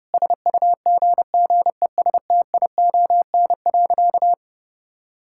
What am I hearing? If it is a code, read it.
Morse code, heard as a keyed tone, about 30 wpm, 700 Hz. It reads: SUGGESTION.
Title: Sentences from Top 1600 Words 30wpm